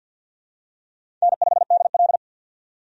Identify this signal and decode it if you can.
Morse code, heard as a keyed tone, about 48 wpm, 700 Hz. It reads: N5DL
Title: USA Call Signs - 4 Characters - 50wpm